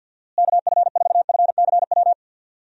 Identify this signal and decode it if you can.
Morse code, heard as a keyed tone, about 48 wpm, 700 Hz. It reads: KF4FCW
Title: USA Call Signs - 6 Characters - Repeated 3x - 50wpm